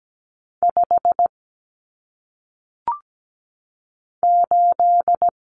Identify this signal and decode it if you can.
Morse code, heard as a keyed tone, about 17 wpm, 700 Hz. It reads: 5 8
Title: Numbers - 1-Digit 17wpm